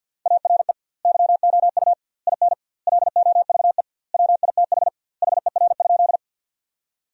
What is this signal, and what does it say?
Morse code, heard as a keyed tone, about 50 wpm, 700 Hz. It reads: ARE YOU IN LOVE WITH HER?
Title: 2x Word-Spacing - Sentences from Top 400 Words - 50wpm